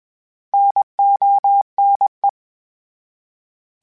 Morse code, heard as a keyed tone, about 20 wpm, 800 Hz. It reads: NONE
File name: Misc / Misc-mp3s / Misc @20